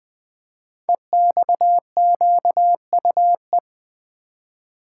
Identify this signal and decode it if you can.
Morse code, heard as a keyed tone, about 20 wpm, 700 Hz. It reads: EXQUE
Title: Code Groups - 5 Letter:Numbers - 20wpm